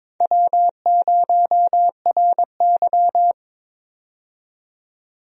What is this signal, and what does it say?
Morse code, heard as a keyed tone, about 22 wpm, 700 Hz. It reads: W0RY
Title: USA Call Signs - 4 Characters - 22wpm